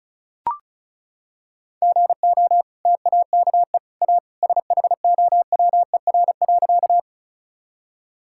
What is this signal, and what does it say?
Morse code, heard as a keyed tone, about 34 wpm, 700 Hz. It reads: GO TAKE A SHOWER.
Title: Sentences from Top 1600 Words 35wpm